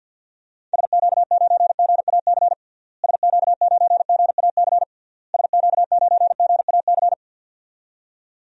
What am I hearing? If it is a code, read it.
Morse code, heard as a keyed tone, about 50 wpm, 700 Hz. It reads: SQ9GAC SQ9GAC SQ9GAC
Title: DXCC-100 Call Signs - 6 Characters - Repeated 3x - 50wpm